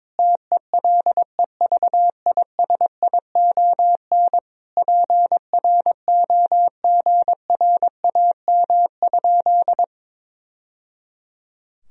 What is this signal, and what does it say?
Morse code, heard as a keyed tone, about 22 wpm, 700 Hz. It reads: TELEVISION PROGRAM?
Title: Sentences from Top 1000 Words - Review Part 4 - 22wpm